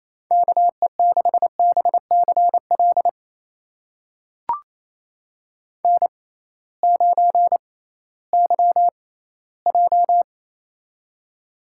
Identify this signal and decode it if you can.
Morse code, heard as a keyed tone, about 27 wpm, spaced as for 16 wpm, 700 Hz. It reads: KE6BCL N9YJ
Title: USA Call Signs - 6x Character Spacing - 28wpm